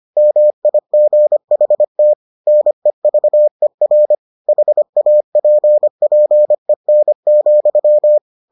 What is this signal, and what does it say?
Morse code, heard as a keyed tone, about 25 wpm, 600 Hz. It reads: MIGHT NEVER HAPPEN,